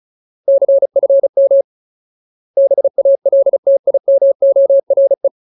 Morse code, heard as a keyed tone, about 34 wpm, 550 Hz. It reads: CFM BALTIMORE
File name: Ultimate Word List / 35WPM Ultimate